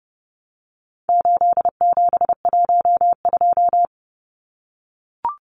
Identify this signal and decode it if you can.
Morse code, heard as a keyed tone, about 30 wpm, 700 Hz. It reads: 8712
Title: Numbers - 4-Digit 30wpm